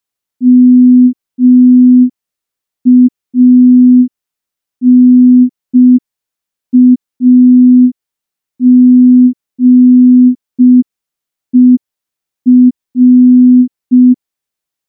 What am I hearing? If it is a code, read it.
Morse code, heard as a keyed tone, about 5 wpm, 250 Hz. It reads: MANAGER
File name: morse / morse-training-05WPM